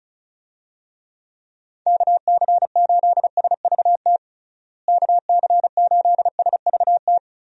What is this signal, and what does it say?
Morse code, heard as a keyed tone, about 36 wpm, 700 Hz. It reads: KC8SVT KC8SVT
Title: USA Call Signs - 6 Characters - Repeated 3x - 35wpm